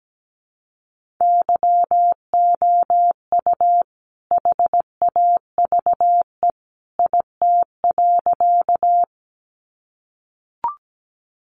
Morse code, heard as a keyed tone, about 17 wpm, 700 Hz. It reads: YOU HAVE IT.
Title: Sentences from Top 100 Words 17wpm